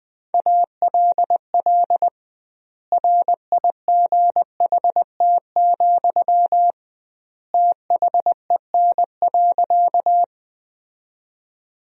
Morse code, heard as a keyed tone, about 20 wpm, 700 Hz. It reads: ALL RIGHT, THEN.
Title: 2x Word-Spacing - Sentences from Top 200 Words - 20wpm